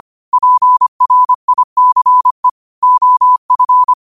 Morse code, heard as a keyed tone, about 25 wpm, 1000 Hz. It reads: PRICE OF